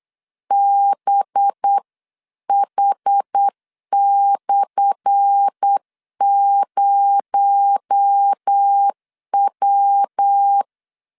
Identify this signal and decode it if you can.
Morse code, heard as a keyed tone, about 8 wpm, 800 Hz. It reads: BH/0W
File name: LEZIONI / 10 SIGNS / 10A SLASH 8wpm 3452